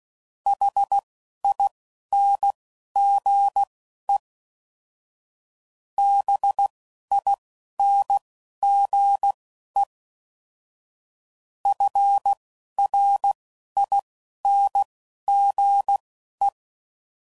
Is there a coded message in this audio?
Yes: Morse code from 0.5 to 16.5 s, heard as a keyed tone, about 16 wpm, spaced as for 12 wpm, 800 Hz. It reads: HINGE BINGE FRINGE